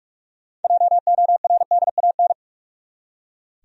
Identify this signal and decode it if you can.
Morse code, heard as a keyed tone, about 45 wpm, 700 Hz. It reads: JORDAN
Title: USA Common Names - 45wpm